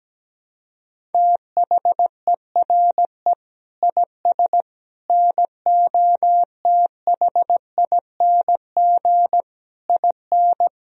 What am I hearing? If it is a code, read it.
Morse code, heard as a keyed tone, about 17 wpm, 700 Hz. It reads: THERE IS NOTHING IN
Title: Sentences from Top 1000 Words - Review Part 1 - 17wpm